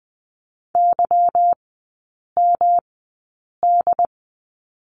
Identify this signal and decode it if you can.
Morse code, heard as a keyed tone, about 20 wpm, spaced as for 8 wpm, 690 Hz. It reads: YMD
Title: 145 Lesson 24e - Letter y - Three Characters 20wpm